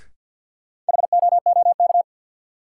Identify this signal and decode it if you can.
Morse code, heard as a keyed tone, about 52 wpm, 700 Hz. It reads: HOOK
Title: Sentences from Top 1600 Words - New Word Drill - 50wpm